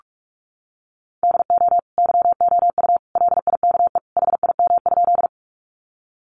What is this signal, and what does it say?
Morse code, heard as a keyed tone, about 44 wpm, 700 Hz. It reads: DO YOU LIKE HIM?
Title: Sentences from Top 100 Words 45wpm